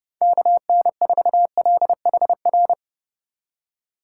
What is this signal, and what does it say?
Morse code, heard as a keyed tone, about 28 wpm, 700 Hz. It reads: KN4LHR